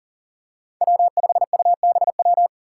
Morse code, heard as a keyed tone, about 40 wpm, 700 Hz. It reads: W5UBW
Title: Five Challenging Letter-Number Combinations 40wpm